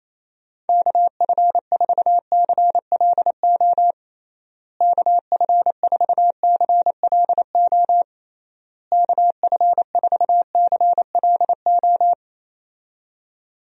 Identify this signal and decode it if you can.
Morse code, heard as a keyed tone, about 28 wpm, 700 Hz. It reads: KF4CLO KF4CLO KF4CLO
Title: USA Call Signs - 6 Characters - Repeated 3x - 28wpm